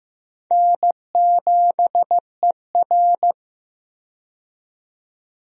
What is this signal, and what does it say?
Morse code, heard as a keyed tone, about 16 wpm, 700 Hz. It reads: N7ER
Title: USA Call Signs - 4 Characters - 15wpm